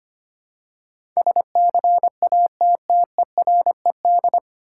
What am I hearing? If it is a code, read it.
Morse code, heard as a keyed tone, about 25 wpm, 700 Hz. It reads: SCATTERED
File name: practice / 25wpm / 9 Letter Words - First 3 Letters Given - 25wpm